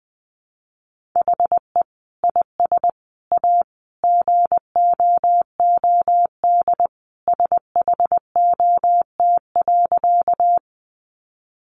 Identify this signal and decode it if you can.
Morse code, heard as a keyed tone, about 20 wpm, 700 Hz. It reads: HE IS A GOOD SHOT.
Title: Sentences from Top 1100 Words 20wpm